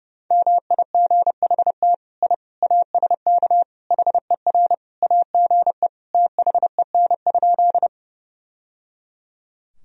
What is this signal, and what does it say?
Morse code, heard as a keyed tone, about 30 wpm, 700 Hz. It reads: MIGHT I ASK HER AGE THEN?
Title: Sentences from Top 1000 Words - Review Part 3 - 30wpm